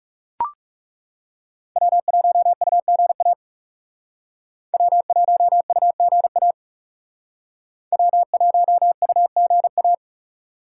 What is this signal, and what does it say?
Morse code, heard as a keyed tone, about 38 wpm, 700 Hz. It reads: W1UGA W1UGA W1UGA